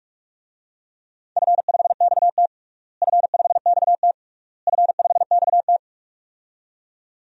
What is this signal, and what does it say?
Morse code, heard as a keyed tone, about 46 wpm, 700 Hz. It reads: F5XT F5XT F5XT
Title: DXCC-100 Call Signs - 4 Characters - Repeated 3x - 45wpm